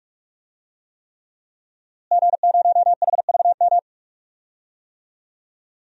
Morse code, heard as a keyed tone, about 46 wpm, 700 Hz. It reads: G0HVM
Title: DXCC-100 Call Signs - 5 Characters - Repeated 3x - 45wpm